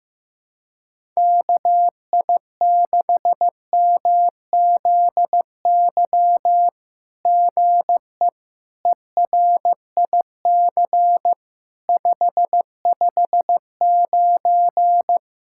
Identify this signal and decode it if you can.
Morse code, heard as a keyed tone, about 15 wpm, 700 Hz. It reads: KI6MZY GE ERIC 559
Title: SOTA QSO Practice 15wpm